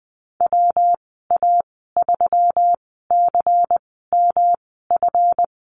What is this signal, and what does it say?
Morse code, heard as a keyed tone, about 20 wpm, spaced as for 14 wpm, 700 Hz. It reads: WA3CMF
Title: 173 Lesson 28g - Number 9 - Call Signs 20wpm